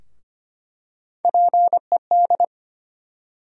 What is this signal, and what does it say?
Morse code, heard as a keyed tone, about 25 wpm, 700 Hz. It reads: PED
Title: USA Call Sign Postfixes - 3 Letters 25wpm